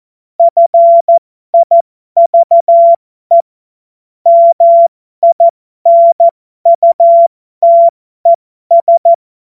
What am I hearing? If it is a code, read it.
Morse code, heard as a keyed tone, about 14 wpm, spaced as for 12 wpm, 680 Hz. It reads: FIVE MINUTES